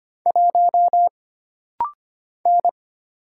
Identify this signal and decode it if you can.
Morse code, heard as a keyed tone, about 25 wpm, 700 Hz. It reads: 1 N
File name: Single Character (Letter:Number:Punctuation:Prosign) - Rapid-Fire - 25wpm